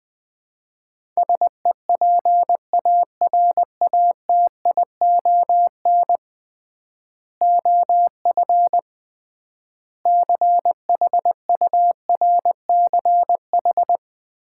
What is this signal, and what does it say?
Morse code, heard as a keyed tone, about 20 wpm, 700 Hz. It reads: SEPARATION OF CHURCH
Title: 3x Word-Spacing - Sets of 5 Words - 20wpm